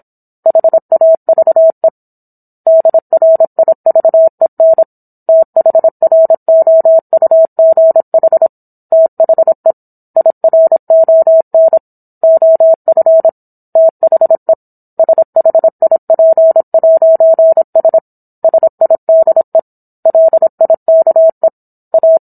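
Morse code, heard as a keyed tone, about 27 wpm, 650 Hz. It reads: HAVE DRIVEN THROUGH THE IRON OF THE SHIP'S SIDE LIKE A
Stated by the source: Morse code practice of a book chapter at 26 WPM.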